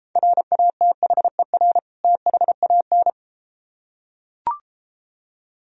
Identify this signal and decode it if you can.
Morse code, heard as a keyed tone, about 33 wpm, 700 Hz. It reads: RATHER THAN